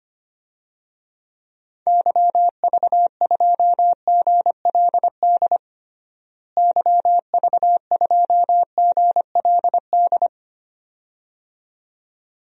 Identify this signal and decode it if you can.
Morse code, heard as a keyed tone, about 25 wpm, 700 Hz. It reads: YV2GLD YV2GLD
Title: DXCC-100 Call Signs - 6 Characters - Repeated 2x - 25wpm